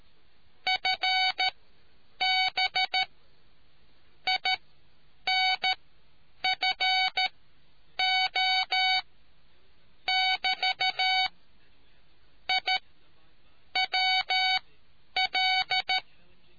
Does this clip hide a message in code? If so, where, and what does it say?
Yes: Morse code from 0.7 to 16.0 s, heard as a keyed tone, about 13 wpm, spaced as for 8 wpm, 750 Hz. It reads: FB INFO <BT> I WL